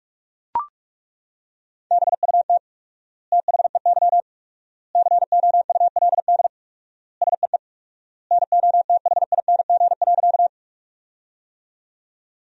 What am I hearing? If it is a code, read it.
Morse code, heard as a keyed tone, about 41 wpm, 690 Hz. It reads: BUT THEY COULD SEE NOTHING.
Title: 4x Word-Spacing - Sentences from Top 500 Words - 45wpm